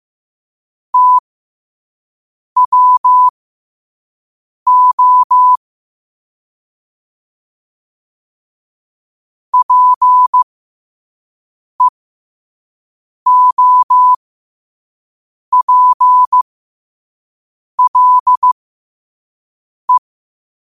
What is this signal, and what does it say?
Morse code, heard as a keyed tone, about 15 wpm, spaced as for 5 wpm, 1000 Hz. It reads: TWO PEOPLE
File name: qotdfiles / 5wpm